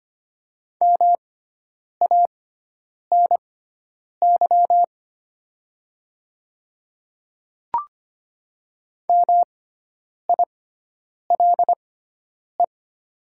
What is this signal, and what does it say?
Morse code, heard as a keyed tone, about 25 wpm, spaced as for 9 wpm, 700 Hz. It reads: MANY MILE
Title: 6x Character Spacing - Top 300 Words - 25wpm